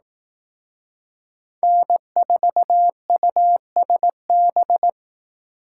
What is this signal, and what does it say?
Morse code, heard as a keyed tone, about 18 wpm, 700 Hz. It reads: N4USB
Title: Speed Racing - USA 1x3 Call Signs 15wpm